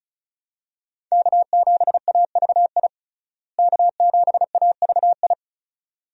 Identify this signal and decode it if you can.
Morse code, heard as a keyed tone, about 34 wpm, 700 Hz. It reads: K7AVI K7AVI
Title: USA Call Signs - 5 Characters - Repeated 2x - 35wpm